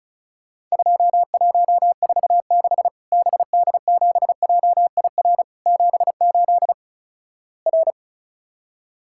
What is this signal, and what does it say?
Morse code, heard as a keyed tone, about 36 wpm, 700 Hz. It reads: 2146 BD7JIR 78 R
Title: Worked All Europe DX Contest - QTC Exchange - 35wpm